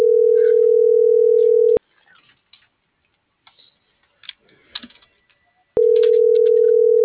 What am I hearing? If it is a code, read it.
Ringback tone (North America), 440 + 480 Hz.